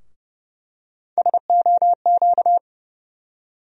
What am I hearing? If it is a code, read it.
Morse code, heard as a keyed tone, about 29 wpm, 700 Hz. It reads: SOQ
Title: USA Call Sign Postfixes - 3 Letters 30wpm